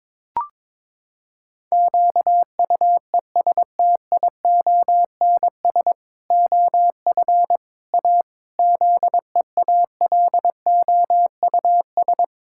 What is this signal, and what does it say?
Morse code, heard as a keyed tone, about 22 wpm, 700 Hz. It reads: QUESTIONS OF A ZEALOUS
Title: Pangram Challenge v2 - 22wpm